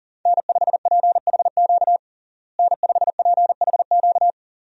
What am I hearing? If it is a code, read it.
Morse code, heard as a keyed tone, about 40 wpm, 700 Hz. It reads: N5PHQ N5PHQ
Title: USA Call Signs - 5 Characters - Repeated 2x - 40wpm